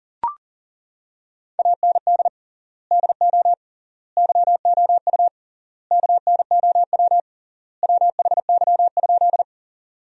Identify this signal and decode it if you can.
Morse code, heard as a keyed tone, about 40 wpm, 700 Hz. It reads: AND DO YOU KNOW WHY?
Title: 3x Word-Spacing - Sentences from Top 200 Words - 40wpm